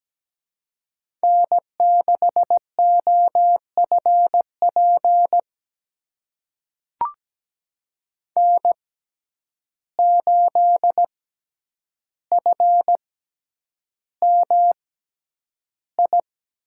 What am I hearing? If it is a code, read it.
Morse code, heard as a keyed tone, about 17 wpm, spaced as for 9 wpm, 700 Hz. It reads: N6OFP N8FMI
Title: USA Call Signs - 5 Characters - 6x Character Spacing - 17wpm